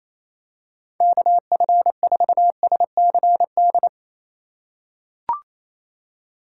Morse code, heard as a keyed tone, about 28 wpm, 700 Hz. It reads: KF4SCD
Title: USA Call Signs - 6 Characters - 2x Character Spacing - 28wpm